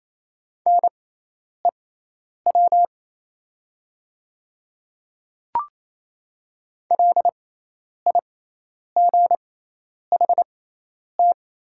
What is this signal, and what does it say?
Morse code, heard as a keyed tone, about 28 wpm, spaced as for 10 wpm, 700 Hz. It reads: NEW LIGHT
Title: 6x Character Spacing - Top 200 Words - 28wpm